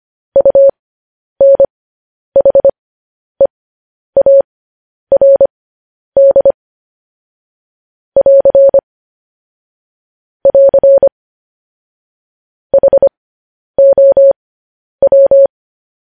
Morse code, heard as a keyed tone, about 25 wpm, spaced as for 10 wpm, 550 Hz. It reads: UNHEARD <AR> <AR> HOW